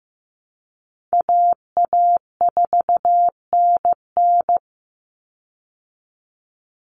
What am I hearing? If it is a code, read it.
Morse code, heard as a keyed tone, about 15 wpm, 700 Hz. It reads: AA4NN
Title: Call Signs Repeated 3x - 15wpm